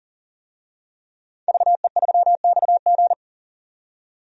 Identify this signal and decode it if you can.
Morse code, heard as a keyed tone, about 40 wpm, 700 Hz. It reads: VE3XG
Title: DXCC-100 Call Signs - 5 Characters - 40wpm